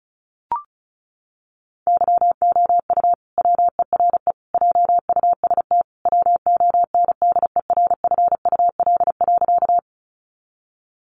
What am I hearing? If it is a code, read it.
Morse code, heard as a keyed tone, about 34 wpm, 700 Hz. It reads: YOU WERE JUST WONDERFUL.
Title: Sentences from Top 1100 Words 35wpm